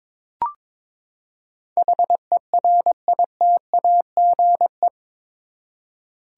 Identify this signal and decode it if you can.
Morse code, heard as a keyed tone, about 22 wpm, 700 Hz. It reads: HERITAGE
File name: practice / 22wpm / Eight Letter Words - 22wpm